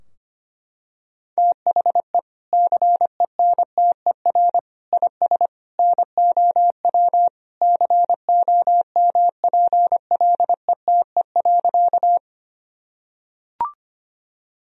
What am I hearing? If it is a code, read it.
Morse code, heard as a keyed tone, about 25 wpm, 700 Hz. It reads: THE CENTER IS NOW COMPLETE.
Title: Sentences from Top 1000 Words - Review Part 2 - 25wpm